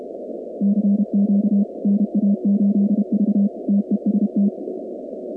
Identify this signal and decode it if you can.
Morse code, heard as a keyed tone, about 32 wpm, 200 Hz. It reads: CQ NA8V TEST
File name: contests / 2019-cw-wpx-kt5j-k5pi- -2330 0-47